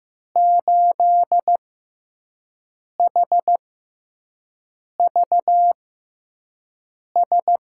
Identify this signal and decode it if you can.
Morse code, heard as a keyed tone, about 16 wpm, spaced as for 5 wpm, 700 Hz. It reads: 8HVS
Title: USA Call Signs - 5 Characters - 6x Character Spacing - 15wpm